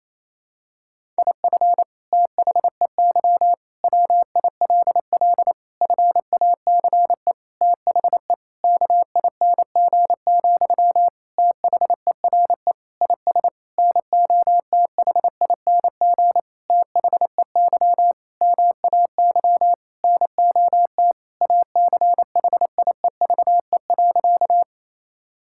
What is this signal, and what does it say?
Morse code, heard as a keyed tone, about 28 wpm, 700 Hz. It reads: IF THEY WILL FACE THE KING, THERE IS NOTHING THEY MAY NOT ACHIEVE.
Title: Sentences from Top 1000 Words - Review Part 6 - 28wpm